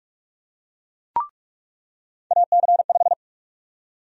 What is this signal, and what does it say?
Morse code, heard as a keyed tone, about 42 wpm, 700 Hz. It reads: AC5
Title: USA Call Sign Prefixes - 45wpm